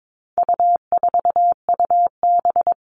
Morse code, heard as a keyed tone, about 22 wpm, 700 Hz. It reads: U4UB
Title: All Cut Numbers - 4-Digit 22wpm